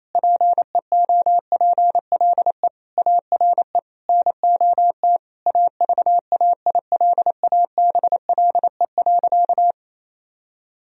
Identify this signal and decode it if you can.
Morse code, heard as a keyed tone, about 28 wpm, 700 Hz. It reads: PEOPLE ARE NOT AVAILABLE.
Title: Sentences from Top 1000 Words - Review Part 3 - 28wpm